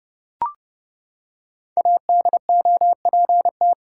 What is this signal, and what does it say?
Morse code, heard as a keyed tone, about 31 wpm, 700 Hz. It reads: ADOPT
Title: Five Letter Words - 30wpm